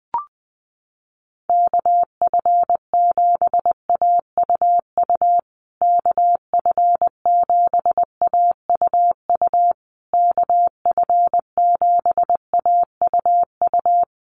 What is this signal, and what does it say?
Morse code, heard as a keyed tone, about 20 wpm, 700 Hz. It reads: KF7AUU KF7AUU KF7AUU
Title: USA Call Signs Repeated 3x 20wpm